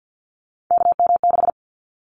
Morse code, heard as a keyed tone, about 50 wpm, 700 Hz. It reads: KM6
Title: Callsign Prefixes for Top 25 DXCC - Repeated 3x - 50wpm